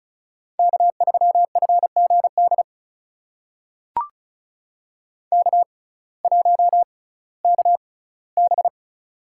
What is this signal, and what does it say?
Morse code, heard as a keyed tone, about 36 wpm, spaced as for 19 wpm, 700 Hz. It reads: K3FGD K1KB
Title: USA Call Signs - 6x Character Spacing - 35wpm